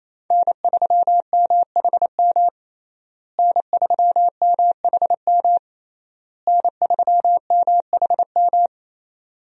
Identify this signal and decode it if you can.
Morse code, heard as a keyed tone, about 27 wpm, 700 Hz. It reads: N3MHM N3MHM N3MHM
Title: USA Call Signs - 5 Characters - Repeated 3x - 28wpm